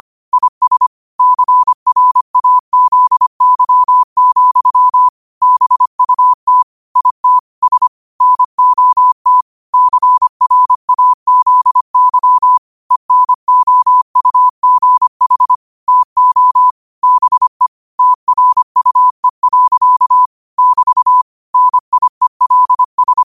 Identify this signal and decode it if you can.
Morse code, heard as a keyed tone, about 25 wpm, 1000 Hz. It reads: IS CRAZY, BUT IT S NOT CRAZY ENOUGH TO BE TRUE. <BT> NIELS
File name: 25wpm